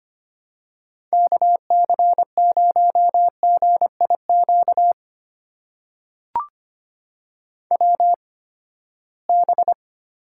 Morse code, heard as a keyed tone, about 25 wpm, spaced as for 17 wpm, 700 Hz. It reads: KC0GIQ WB
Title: USA Call Signs - 8x Character Spacing - 25wpm